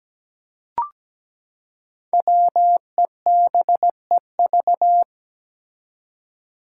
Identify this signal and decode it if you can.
Morse code, heard as a keyed tone, about 17 wpm, 700 Hz. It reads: WEBEV
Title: Five Challenging Letter-Number Combinations 17wpm